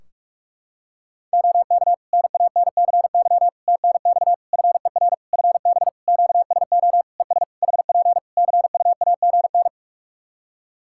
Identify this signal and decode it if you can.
Morse code, heard as a keyed tone, about 46 wpm, 700 Hz. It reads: OK NANCY TNX FER FB QSO ES HP CUAGN
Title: Practice QSO - Exchange 4 - Ending - 45wpm